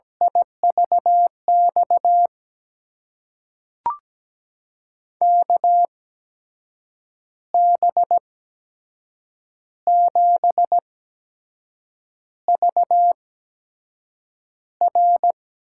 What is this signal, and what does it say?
Morse code, heard as a keyed tone, about 17 wpm, spaced as for 6 wpm, 700 Hz. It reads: IVX KB7VR
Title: USA Call Signs - 6 Characters - 8x Character Spacing - 17wpm